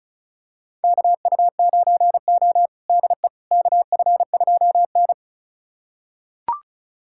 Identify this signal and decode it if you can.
Morse code, heard as a keyed tone, about 36 wpm, 700 Hz. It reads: KU9O DE KF2N
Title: USA Callsign DE Callsign - 4 Characters - 35wpm